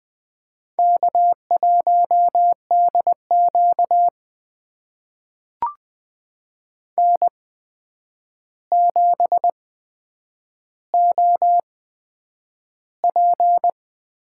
Morse code, heard as a keyed tone, about 20 wpm, spaced as for 9 wpm, 700 Hz. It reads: K1DQ N7OP
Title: USA Call Signs - 4 Characters - 8x Character Spacing - 20wpm